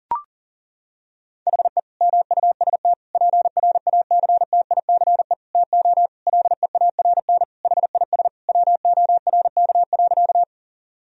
Morse code, heard as a keyed tone, about 37 wpm, 700 Hz. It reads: HE MUST PRACTICE TO LEARN HIS WORK.
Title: Sentences from Top 1000 Words - Review Part 4 - 40wpm